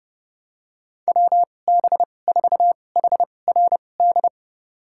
Morse code, heard as a keyed tone, about 30 wpm, spaced as for 22 wpm, 700 Hz. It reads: WB4HRD
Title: USA Call Signs - 6 Characters - 2x Character Spacing - 30wpm